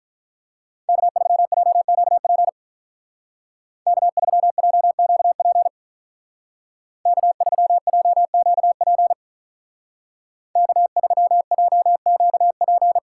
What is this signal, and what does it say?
Morse code, heard as a keyed tone, about 43 wpm, 700 Hz. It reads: K3JQP K3JQP K3JQP K3JQP
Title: Speed Racing - USA 1x3 Call Signs 35wpm